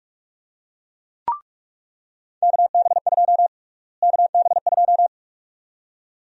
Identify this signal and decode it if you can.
Morse code, heard as a keyed tone, about 46 wpm, 700 Hz. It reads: KB2 KB2
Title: USA Call Sign Prefixes - Repeated 2x - 45wpm